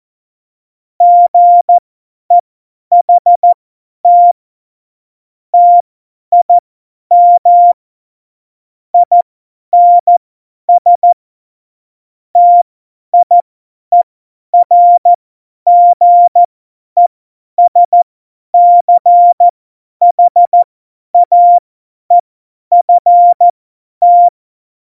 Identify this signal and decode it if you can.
Morse code, heard as a keyed tone, about 14 wpm, spaced as for 10 wpm, 700 Hz. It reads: GEHT TIM INS TIERGESCHAEFT
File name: Morsekurs / Witze / Witz 16. Oktober 2018, 50 F70